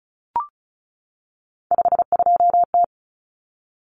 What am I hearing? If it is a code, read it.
Morse code, heard as a keyed tone, about 34 wpm, 700 Hz. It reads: 52T
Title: Common Cut Numbers - 3-Digit 35wpm